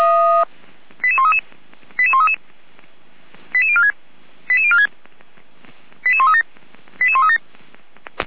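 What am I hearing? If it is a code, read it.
Five-tone selective call (ZVEI), heard as quick runs of five tones, tones of about 70 ms. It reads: 88133 88133 88046 88046 88137 88137
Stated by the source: SDR-received ZVEI five-tone call 88335, recorded by Feuerwehr Orlamünde.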